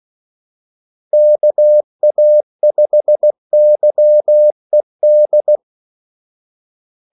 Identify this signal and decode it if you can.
Morse code, heard as a keyed tone, about 16 wpm, 600 Hz. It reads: KA5YED
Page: Kaldesignaler 16 wpm